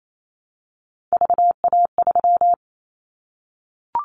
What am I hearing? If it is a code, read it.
Morse code, heard as a keyed tone, about 28 wpm, 700 Hz. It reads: VA3
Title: Callsign Prefixes for Top 100 DXCC - 28wpm